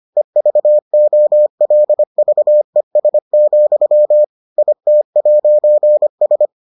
Morse code, heard as a keyed tone, about 25 wpm, 600 Hz. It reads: EVOLVES, IT'S